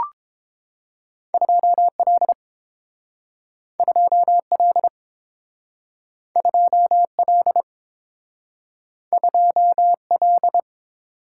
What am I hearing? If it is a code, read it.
Morse code, heard as a keyed tone, about 30 wpm, 700 Hz. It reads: 2L 2L 2L 2L